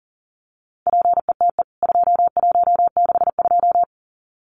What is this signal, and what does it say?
Morse code, heard as a keyed tone, about 40 wpm, 700 Hz. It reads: PETE 2162
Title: CWT Exchange 40wpm